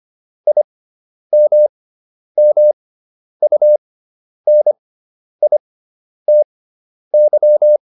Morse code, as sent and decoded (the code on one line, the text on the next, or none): .. -- -- ..- -. .. - -.--
IMMUNITY